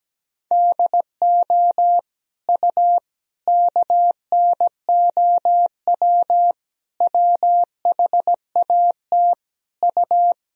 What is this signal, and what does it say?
Morse code, heard as a keyed tone, about 17 wpm, 700 Hz. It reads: DO U KNOW WHAT U
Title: Sentences from Top 100 Words and Abbreviations 17wpm